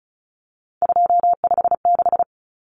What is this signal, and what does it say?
Morse code, heard as a keyed tone, about 34 wpm, 700 Hz. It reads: 256
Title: Common Cut Numbers - 3-Digit 35wpm